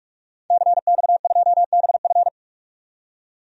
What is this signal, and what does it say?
Morse code, heard as a keyed tone, about 43 wpm, 700 Hz. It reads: CX2BF
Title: DXCC-100 Call Signs - 5 Characters - 45wpm